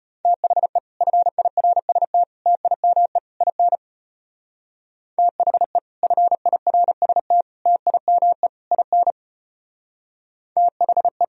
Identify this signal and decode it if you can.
Morse code, heard as a keyed tone, about 32 wpm, 700 Hz. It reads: THE FIRST TIME IN THE FIRST TIME IN THE